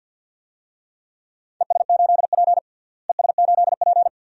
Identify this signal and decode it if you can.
Morse code, heard as a keyed tone, about 50 wpm, 700 Hz. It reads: ES8P ES8P
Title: DXCC-100 Call Signs - 4 Characters - Repeated 2x - 50wpm